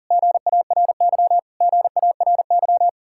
Morse code, heard as a keyed tone, about 52 wpm, 700 Hz. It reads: GARY GARY